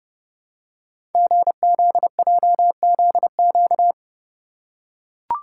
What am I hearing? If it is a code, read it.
Morse code, heard as a keyed tone, about 30 wpm, 690 Hz. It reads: GZJZQ
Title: Code Groups - 5 Letter:Numbers - 30wpm